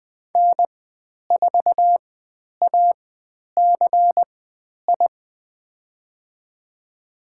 Morse code, heard as a keyed tone, about 20 wpm, spaced as for 10 wpm, 700 Hz. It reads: N4ACI
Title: 095 Lesson 16g - Letter c - Call Signs 20wpm